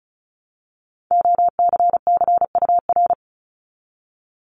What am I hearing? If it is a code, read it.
Morse code, heard as a keyed tone, about 36 wpm, 700 Hz. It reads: OCCUR
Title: Multi-letter Phonemes 35wpm